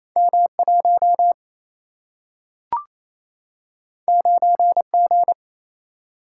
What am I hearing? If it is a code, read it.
Morse code, heard as a keyed tone, about 28 wpm, 700 Hz. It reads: M1 9G
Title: Single-Letter Single-Number 28wpm